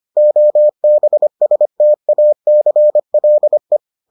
Morse code, heard as a keyed tone, about 25 wpm, 600 Hz. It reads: OBSTACLE